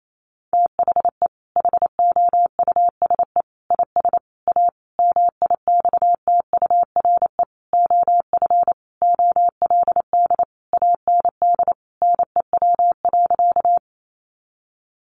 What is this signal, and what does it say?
Morse code, heard as a keyed tone, about 28 wpm, 700 Hz. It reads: THE HOUSE IS A MIXTURE OF OLD AND NEW.
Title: Sentences from Top 1500 Words 28wpm